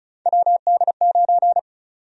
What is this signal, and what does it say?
Morse code, heard as a keyed tone, about 36 wpm, 700 Hz. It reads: WD9
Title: USA Call Sign Prefixes - 35wpm